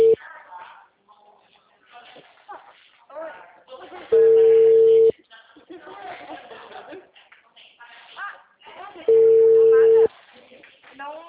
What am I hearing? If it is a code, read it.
Ringback tone (China, 450 Hz), with speech.
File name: out- -2016- - - .57654